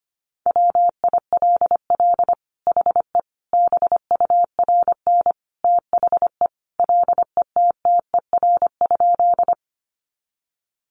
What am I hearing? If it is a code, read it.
Morse code, heard as a keyed tone, about 25 wpm, 700 Hz. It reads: WILL HE BURN THE LETTER?
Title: Sentences from Top 1300 Words 25wpm